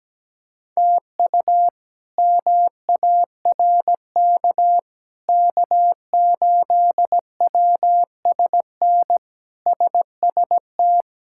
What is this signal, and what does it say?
Morse code, heard as a keyed tone, about 17 wpm, 700 Hz. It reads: TU MARK K8WSN SST
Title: K1USN SST QSO Practice - 17wpm